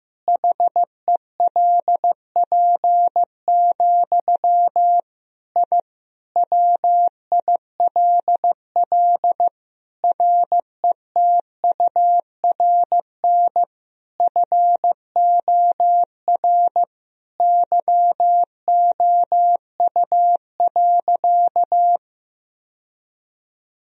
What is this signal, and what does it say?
Morse code, heard as a keyed tone, about 15 wpm, 700 Hz. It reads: HELP, I WILL RETURN FOR YOU.
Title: Sentences from Top 1000 Words - Review Part 1 - 15wpm